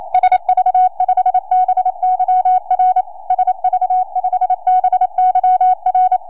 Morse code, heard as a keyed tone, about 27 wpm, 740 Hz. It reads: SV5BYR SV5BYR
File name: 4m